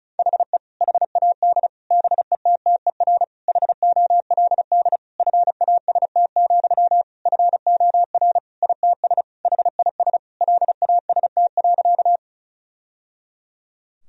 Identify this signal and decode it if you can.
Morse code, heard as a keyed tone, about 36 wpm, 700 Hz. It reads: HE HAD BETTER HOLD FAST, FOR ITS HIS LAST.
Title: Sentences from Top 1000 Words - Review Part 2 - 35wpm